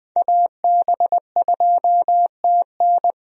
Morse code, heard as a keyed tone, about 20 wpm, 700 Hz. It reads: AB2TN
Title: USA Call Signs - 5 Characters - 20wpm